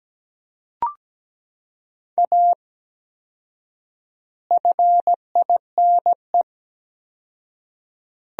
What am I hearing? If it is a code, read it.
Morse code, heard as a keyed tone, about 17 wpm, 700 Hz. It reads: A FINE
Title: 4x Word-Spacing - Sentences from Top 500 Words - 17wpm